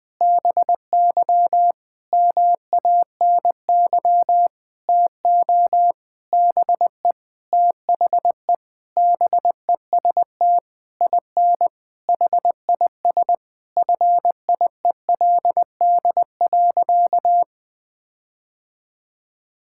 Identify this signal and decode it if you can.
Morse code, heard as a keyed tone, about 21 wpm, 700 Hz. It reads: BY MANY TO BE THE BEST IN HIS FIELD.
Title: Sentences with 7-Word Phrases - 20wpm